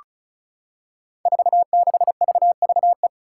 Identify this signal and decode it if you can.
Morse code, heard as a keyed tone, about 36 wpm, 700 Hz. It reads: 46VVE
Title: Five Challenging Letter-Number Combinations 35wpm